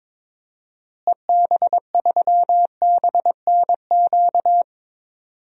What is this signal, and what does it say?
Morse code, heard as a keyed tone, about 22 wpm, 700 Hz. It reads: EB3BNQ
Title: DXCC-100 Call Signs - 6 Characters - Repeated 2x - 22wpm